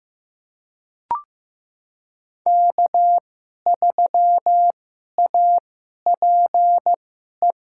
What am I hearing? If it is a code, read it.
Morse code, heard as a keyed tone, about 15 wpm, spaced as for 11 wpm, 700 Hz. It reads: K3APE
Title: USA Call Signs - 5 Characters - 2x Character Spacing - 15wpm